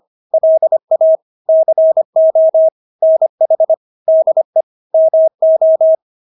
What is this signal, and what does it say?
Morse code, heard as a keyed tone, about 25 wpm, 640 Hz. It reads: LA CO NH DE MO
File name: State-Abbreviations-25